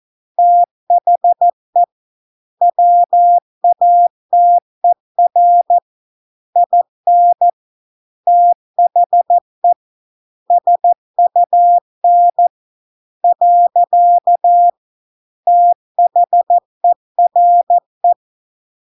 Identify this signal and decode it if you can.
Morse code, heard as a keyed tone, about 14 wpm, 700 Hz. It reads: THE WATER IN THE SUN . THERE